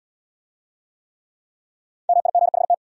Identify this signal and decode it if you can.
Morse code, heard as a keyed tone, about 72 wpm, 700 Hz. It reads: MIGHT